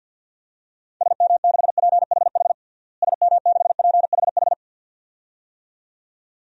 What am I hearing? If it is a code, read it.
Morse code, heard as a keyed tone, about 53 wpm, 700 Hz. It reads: SM6PHH SM6PHH
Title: DXCC-100 Call Signs - 6 Characters - Repeated 2x - 50wpm